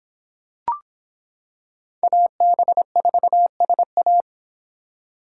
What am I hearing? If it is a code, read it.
Morse code, heard as a keyed tone, about 25 wpm, 700 Hz. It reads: AB4SA